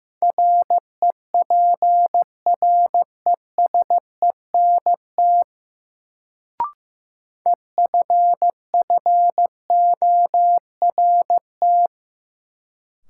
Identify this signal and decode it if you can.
Morse code, heard as a keyed tone, about 15 wpm, 700 Hz. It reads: REPRESENT EFFORT
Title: Sentences from Top 1200 Words - New Word Drill - 15wpm